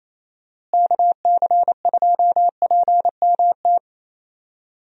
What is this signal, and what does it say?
Morse code, heard as a keyed tone, about 28 wpm, 700 Hz. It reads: KC2PMT
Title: USA Call Signs - 6 Characters - Repeated 2x - 28wpm